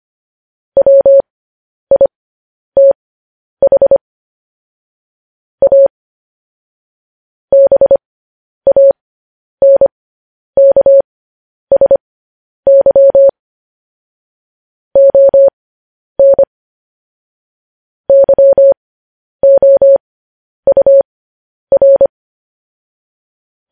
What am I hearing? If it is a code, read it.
Morse code, heard as a keyed tone, about 25 wpm, spaced as for 10 wpm, 550 Hz. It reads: WITH A BANKSY ON YOUR
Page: news headlines in morse code at 10 wpm